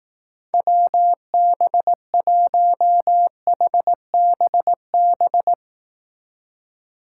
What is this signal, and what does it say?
Morse code, heard as a keyed tone, about 18 wpm, 700 Hz. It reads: WB1HBB
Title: Speed Racing - USA 2x3 Call Signs 15wpm